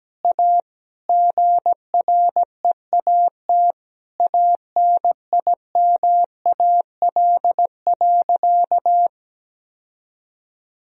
Morse code, heard as a keyed tone, about 17 wpm, 700 Hz. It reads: A GREAT ANIMAL.
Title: Sentences from Top 1000 Words - Review Part 1 - 17wpm